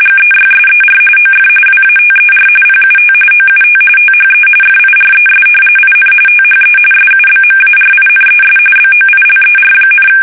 RTTY, about 45.45 baud. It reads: GEYSER OF WATER PUT OUT THE FIRES IN THE SUBMARINES BOILER,
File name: fsk850 / usnavy3